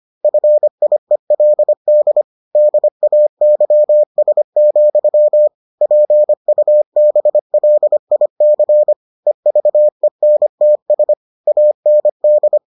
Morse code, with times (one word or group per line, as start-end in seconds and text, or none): 0.2-2.2 FIELD
2.6-5.5 DAYS,
5.8-8.9 PUBLIC
9.3-11.1 EVENTS
11.5-12.6 AND